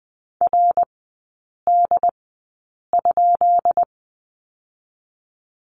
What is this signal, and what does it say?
Morse code, heard as a keyed tone, about 20 wpm, spaced as for 8 wpm, 690 Hz. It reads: RD?
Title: 205 Lesson 33e - Letter j - Three Characters 20wpm